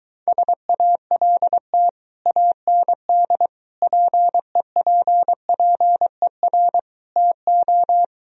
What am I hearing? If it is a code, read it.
Morse code, heard as a keyed tone, about 23 wpm, 700 Hz. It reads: SALT AND PEPPER TO